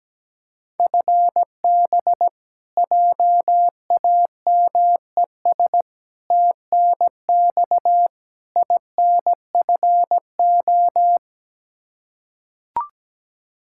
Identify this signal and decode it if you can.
Morse code, heard as a keyed tone, about 17 wpm, 700 Hz. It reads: FB JAMES TNX INFO
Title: Practice QSO - Exchange 3 - More Info - 17wpm